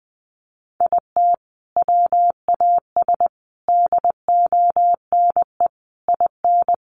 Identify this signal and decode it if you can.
Morse code, heard as a keyed tone, about 20 wpm, 700 Hz. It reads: IT WAS DONE IN
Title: Sentences from Top 1400 Words 20wpm